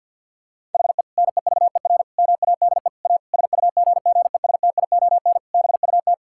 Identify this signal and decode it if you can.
Morse code, heard as a keyed tone, about 50 wpm, 700 Hz. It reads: HE NEVER MADE A SUGGESTION BUT
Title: Sentences from Top 1600 Words 50wpm